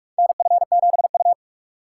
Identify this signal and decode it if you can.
Morse code, heard as a keyed tone, about 46 wpm, 700 Hz. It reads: NF7U
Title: USA Call Signs - 4 Characters - 2x Character Spacing - 45wpm